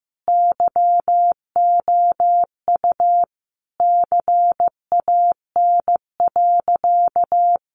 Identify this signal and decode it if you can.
Morse code, heard as a keyed tone, about 15 wpm, 700 Hz. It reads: YOU CAN.
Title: Sentences from Top 1400 Words 15wpm